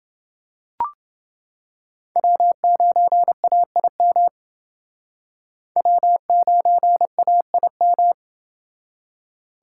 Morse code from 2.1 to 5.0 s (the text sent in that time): W9AIM